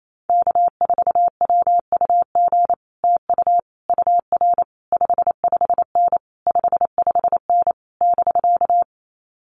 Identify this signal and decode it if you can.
Morse code, heard as a keyed tone, about 28 wpm, 700 Hz. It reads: K4WUG TU UR 55N 55N <BK>
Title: Common QSO Phrases Encore 28wpm